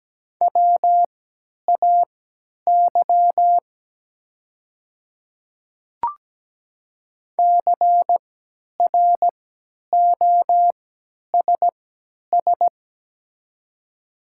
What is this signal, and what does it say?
Morse code, heard as a keyed tone, about 17 wpm, spaced as for 10 wpm, 700 Hz. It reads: WAY CROSS
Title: 3x Character Spacing - Top 300 Words - 17wpm